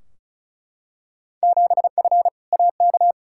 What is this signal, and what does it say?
Morse code, heard as a keyed tone, about 36 wpm, 690 Hz. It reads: 7F AK
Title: Field Day Exchanges Encore - 35wpm